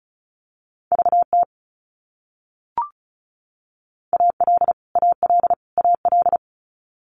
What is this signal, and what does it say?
Morse code, heard as a keyed tone, about 34 wpm, 700 Hz. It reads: VT AL AL AL
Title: US States Repeated 3x 35wpm